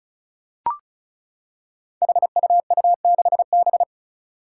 Morse code, heard as a keyed tone, about 34 wpm, 700 Hz. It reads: HUU6B
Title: Code Groups - 5 Letter:Numbers - 35wpm